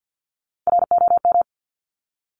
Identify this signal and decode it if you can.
Morse code, heard as a keyed tone, about 50 wpm, 700 Hz. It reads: ROM
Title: Top Three Letter Combinations in Words 50wpm